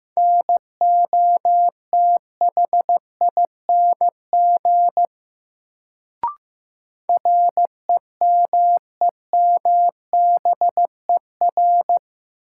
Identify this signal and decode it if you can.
Morse code, heard as a keyed tone, about 15 wpm, 700 Hz. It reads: NOTHING REMEMBER
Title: Sets of 4 Words - New Word Drill - 15wpm